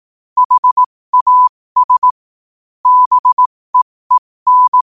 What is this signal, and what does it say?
Morse code, heard as a keyed tone, about 18 wpm, spaced as for 15 wpm, 1000 Hz. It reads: HAS BEEN
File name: qotdfiles / 15wpm